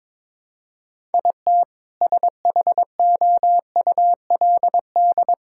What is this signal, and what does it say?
Morse code, heard as a keyed tone, about 22 wpm, 700 Hz. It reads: IT SHOULD
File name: practice / 22wpm / Sets of 7 Words - 22wpm